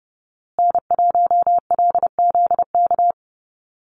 Morse code, heard as a keyed tone, about 30 wpm, 700 Hz. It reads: N1LZK
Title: USA Call Signs Repeated 2x 30wpm